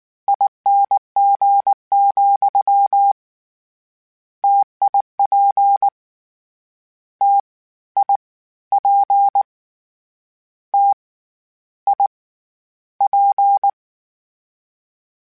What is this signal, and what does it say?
Morse code, heard as a keyed tone, about 19 wpm, 800 Hz. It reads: ING, TIP T I P T I P
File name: Variable Spacing / Variable Spacing-mp3s / T AOEs @18